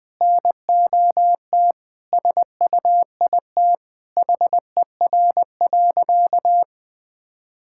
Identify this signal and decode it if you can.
Morse code, heard as a keyed tone, about 20 wpm, 700 Hz. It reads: NOT SUIT HER.
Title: Sentences from Top 1600 Words 20wpm